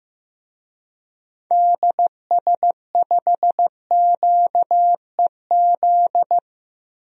Morse code, heard as a keyed tone, about 15 wpm, 700 Hz. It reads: DS5QEZ
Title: DXCC-100 Call Signs - 6 Characters - 15wpm